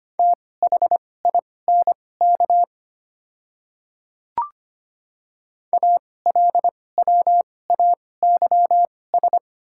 Morse code, heard as a keyed tone, about 25 wpm, spaced as for 18 wpm, 690 Hz. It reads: THINK ALWAYS
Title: 2x Character Spacing - Top 300 Words - 25wpm